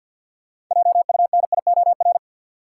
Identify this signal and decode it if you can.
Morse code, heard as a keyed tone, about 46 wpm, 700 Hz. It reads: JUNIOR
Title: Sentences from Top 1500 Words - New Word Drill Alt - 50wpm